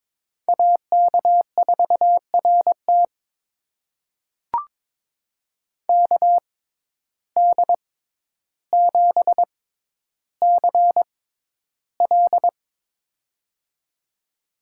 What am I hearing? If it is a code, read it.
Morse code, heard as a keyed tone, about 22 wpm, spaced as for 11 wpm, 700 Hz. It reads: AK4RT KD7CL
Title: USA Call Signs - 5 Characters - 6x Character Spacing - 22wpm